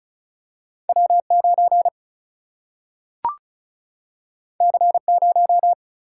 Morse code, heard as a keyed tone, about 36 wpm, 700 Hz. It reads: W 9 C 0
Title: Single-Letter Single-Number 35wpm